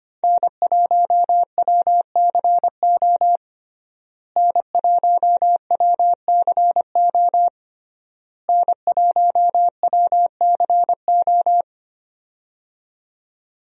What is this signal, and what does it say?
Morse code, heard as a keyed tone, about 25 wpm, 700 Hz. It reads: N1WCO N1WCO N1WCO
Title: USA Call Signs - 5 Characters - Repeated 3x - 25wpm